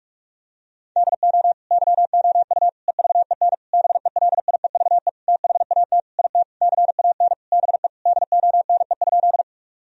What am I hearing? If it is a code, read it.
Morse code, heard as a keyed tone, about 46 wpm, 700 Hz. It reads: DO YOU EVEN BELIEVE THAT IT CAN BE DONE?
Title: Sentences from Top 1000 Words - Review Part 1 - 45wpm